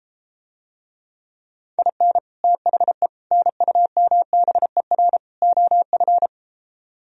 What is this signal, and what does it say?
Morse code, heard as a keyed tone, about 33 wpm, 700 Hz. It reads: IN THE NUMBER OF